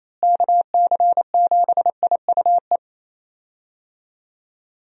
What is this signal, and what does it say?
Morse code, heard as a keyed tone, about 28 wpm, 700 Hz. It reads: KC7IUE
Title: USA Call Signs - 6 Characters - Repeated 3x - 28wpm